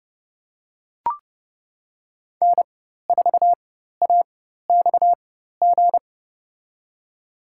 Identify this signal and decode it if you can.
Morse code, heard as a keyed tone, about 30 wpm, spaced as for 14 wpm, 700 Hz. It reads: N4AXG
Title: USA Call Signs - 5 Characters - 4x Character Spacing - 30wpm